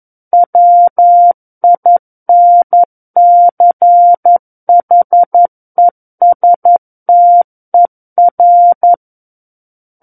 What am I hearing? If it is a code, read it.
Morse code, heard as a keyed tone, about 11 wpm, 700 Hz. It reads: WINCHESTER
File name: 11 WPM